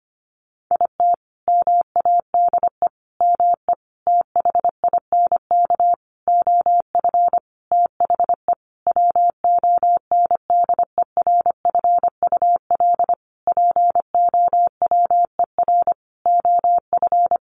Morse code, heard as a keyed tone, about 25 wpm, 690 Hz. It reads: IT MADE ME THINK OF THE WONDERFUL POWER OF
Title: Sentences from Top 1100 Words 25wpm